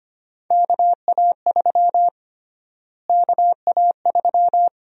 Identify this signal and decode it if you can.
Morse code, heard as a keyed tone, about 25 wpm, 700 Hz. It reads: KA3 KA3
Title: USA Call Sign Prefixes - Repeated 2x - 25wpm